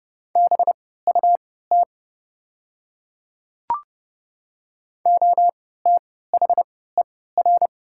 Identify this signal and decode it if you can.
Morse code, heard as a keyed tone, about 30 wpm, spaced as for 17 wpm, 700 Hz. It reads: BUT OTHER
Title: 3x Character Spacing - Top 100 Words - 30wpm